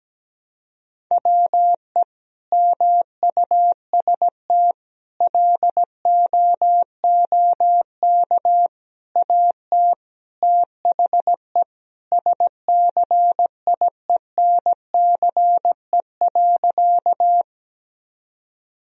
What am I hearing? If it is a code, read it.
Morse code, heard as a keyed tone, about 17 wpm, 700 Hz. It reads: WE MUST LOOK AT THE SCIENCE.
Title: Sentences from Top 1000 Words - Review Part 2 - 17wpm